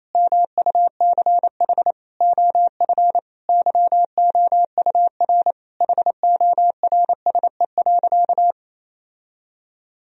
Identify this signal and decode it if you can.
Morse code, heard as a keyed tone, about 28 wpm, 700 Hz. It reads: MUCH OF YOUR HORSE.
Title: Sentences from Top 1000 Words - Review Part 2 - 28wpm